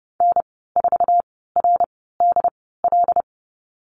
Morse code, heard as a keyed tone, about 30 wpm, spaced as for 17 wpm, 700 Hz. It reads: N4RDL
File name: Lesson 14g - Number 2 - Call Signs 30wpm